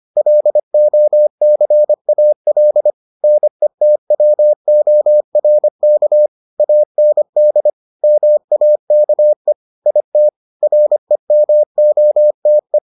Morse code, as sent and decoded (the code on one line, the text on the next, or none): .-.. --- -.-. .- .-.. / -. - .-- --- .-. -.- / .- -. -.. / -- .- -.- . / .. - / .-. -- --- - .
LOCAL NETWORK AND MAKE IT REMOTE